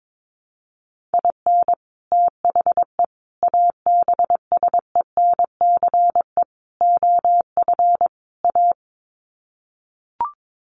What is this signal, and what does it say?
Morse code, heard as a keyed tone, about 22 wpm, 700 Hz. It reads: IN THE ABSENCE OF A
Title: Sets of 5 Words Encore 22wpm